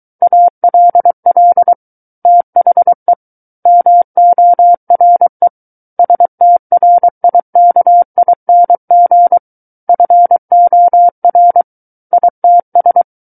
Morse code, heard as a keyed tone, about 23 wpm, 700 Hz. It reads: ALL THE MORE STRIKING FOR ITS
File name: Book One / 14-Chapter 14 - 23 WPM